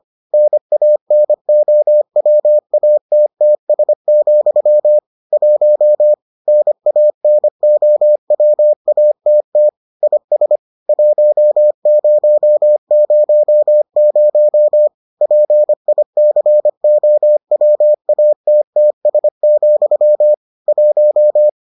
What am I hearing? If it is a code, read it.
Morse code, heard as a keyed tone, about 25 wpm, 600 Hz. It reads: NANOWATTS, 1 NANOWATT IS 1000 PICOWATTS, 1